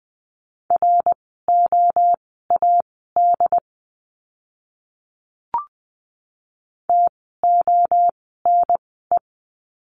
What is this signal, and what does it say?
Morse code, heard as a keyed tone, about 20 wpm, spaced as for 14 wpm, 700 Hz. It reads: ROAD TONE
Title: 068 Lesson 12f - Letter d - Words 20wpm